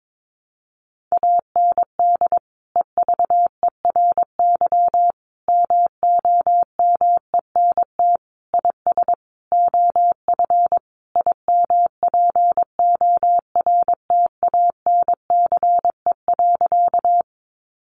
Morse code, heard as a keyed tone, about 22 wpm, 700 Hz. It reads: AND EVERY MOMENT IS OF IMPORTANCE.
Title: Sentences from Top 1200 Words 22wpm